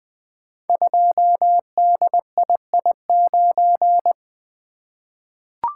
Morse code, heard as a keyed tone, about 20 wpm, 700 Hz. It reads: 2DII9
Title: Code Groups - 5 Letter:Numbers - 20wpm